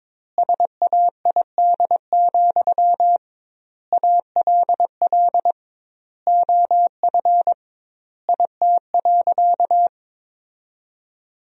Morse code, heard as a keyed tone, about 22 wpm, 700 Hz. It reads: SAID, ALL OF IT.
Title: 2x Word-Spacing - Sentences from Top 500 Words - 22wpm